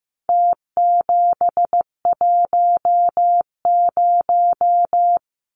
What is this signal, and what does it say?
Morse code, heard as a keyed tone, about 15 wpm, 700 Hz. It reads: T710
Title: Numbers - Contest Style - 4-Digit 15wpm